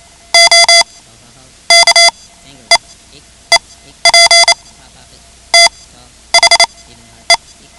Morse code, heard as a keyed tone, about 28 wpm, spaced as for 11 wpm, 2050 Hz. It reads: O KEEP THE
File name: ham radio / CodeQuick / Code Quick 10